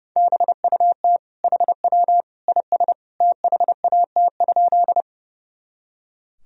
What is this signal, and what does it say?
Morse code, heard as a keyed tone, about 32 wpm, 700 Hz. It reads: BUT HW IS THAT?
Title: Sentences from Top 100 Words and Abbreviations 30wpm